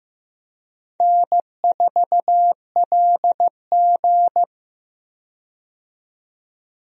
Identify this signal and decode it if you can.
Morse code, heard as a keyed tone, about 15 wpm, 700 Hz. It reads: N4LG
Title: USA Call Signs - 4 Characters - 15wpm